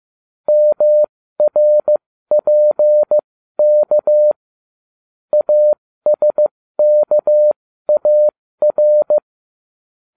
Morse code, heard as a keyed tone, about 15 wpm, spaced as for 13 wpm, 600 Hz. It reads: MRPK ASKAR